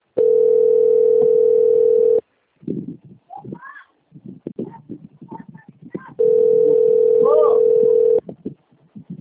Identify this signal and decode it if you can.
Ringback tone (North America), 440 + 480 Hz.
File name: out- - - - - .21533